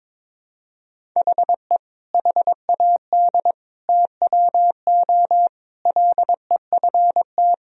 Morse code, heard as a keyed tone, about 22 wpm, 700 Hz. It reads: HE HAD TWO LEFT
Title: Sentences from Top 1000 Words - Review Part 1 - 22wpm